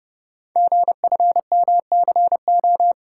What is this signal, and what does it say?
Morse code, heard as a keyed tone, about 30 wpm, 700 Hz. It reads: GFMCO
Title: Code Groups - 5 Letters - 30wpm